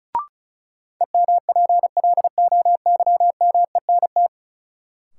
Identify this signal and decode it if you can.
Morse code, heard as a keyed tone, about 36 wpm, 700 Hz. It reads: EMPLOYMENT
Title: Sentences from Top 1500 Words - New Word Drill Alt - 35wpm